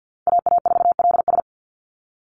Morse code, heard as a keyed tone, about 49 wpm, 700 Hz. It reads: AA4LS
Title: Call Signs Repeated 2x - 50wpm